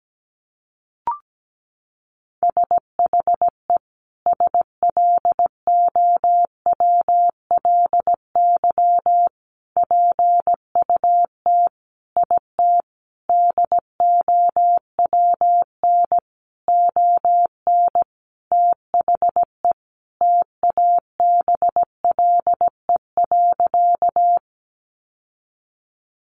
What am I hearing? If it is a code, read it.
Morse code, heard as a keyed tone, about 17 wpm, 700 Hz. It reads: SHE SLOWLY PUT IT DOWN ON THE TABLE.
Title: Sentences from Top 1100 Words 17wpm